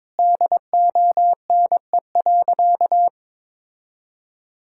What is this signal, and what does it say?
Morse code, heard as a keyed tone, about 22 wpm, 700 Hz. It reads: DONE.
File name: practice / 22wpm / Sentences with 3-Word Phrases - 22wpm